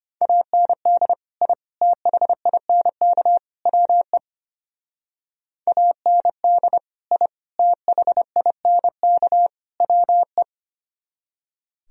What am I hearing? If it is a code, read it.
Morse code, heard as a keyed tone, about 29 wpm, 700 Hz. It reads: AND I THINK WE AND I THINK WE
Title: Speed-Racing - Sets of 4 Words 25wpm